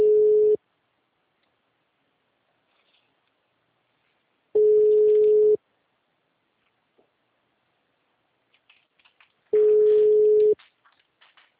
Ringback tone (Europe, 425 Hz).